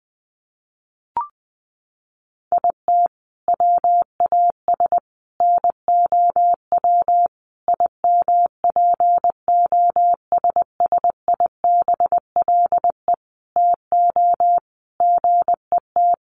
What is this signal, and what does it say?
Morse code, heard as a keyed tone, about 20 wpm, 700 Hz. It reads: IT WAS NOW IMPOSSIBLE TO GET
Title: Sentences from Top 1100 Words 20wpm